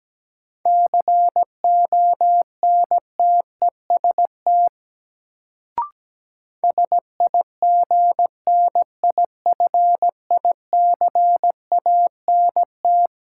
Morse code, heard as a keyed tone, about 17 wpm, 700 Hz. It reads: CONTEST SIGNIFICANT
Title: Sentences from Top 1600 Words - New Word Drill Alt - 17wpm